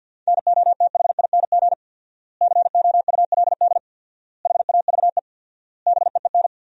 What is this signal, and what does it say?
Morse code, heard as a keyed tone, about 50 wpm, 700 Hz. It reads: NOTHING COULD HAVE BEEN
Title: 4x Word-Spacing - Sentences from Top 500 Words - 50wpm